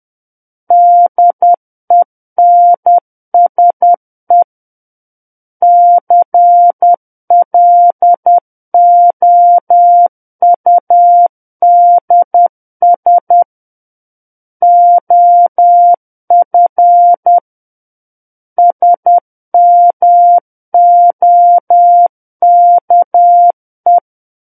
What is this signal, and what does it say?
Morse code, heard as a keyed tone, about 10 wpm, 700 Hz. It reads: DENSE CLOUDS OF SMOKE
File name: Book One / 01-Chapter 1 - 10 WPM